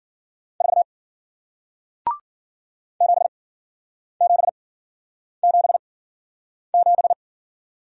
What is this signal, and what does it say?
Morse code, heard as a keyed tone, about 54 wpm, 700 Hz. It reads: 4 7 7 7 7